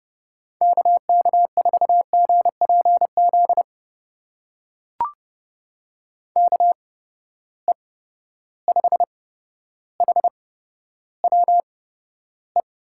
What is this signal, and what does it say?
Morse code, heard as a keyed tone, about 30 wpm, spaced as for 13 wpm, 700 Hz. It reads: KK4GPZ KE5HWE
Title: USA Call Signs - 6 Characters - 8x Character Spacing - 30wpm